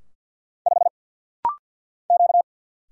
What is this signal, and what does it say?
Morse code, heard as a keyed tone, about 50 wpm, 690 Hz. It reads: 5 Q
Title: Single Letter-Number Alt - Rapid-Fire 50wpm